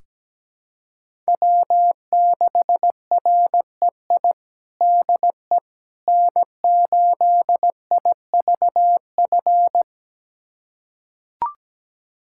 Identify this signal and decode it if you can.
Morse code, heard as a keyed tone, about 17 wpm, 700 Hz. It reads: W6REI DE N8IVF
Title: USA Callsign DE Callsign 17wpm